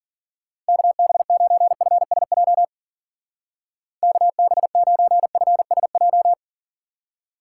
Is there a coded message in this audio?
Yes: Morse code from 0.7 to 6.3 s, heard as a keyed tone, about 52 wpm, 700 Hz. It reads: KB9FSJ KB9FSJ